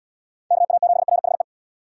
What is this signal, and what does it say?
Morse code, heard as a keyed tone, about 79 wpm, 700 Hz. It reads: KA7RHE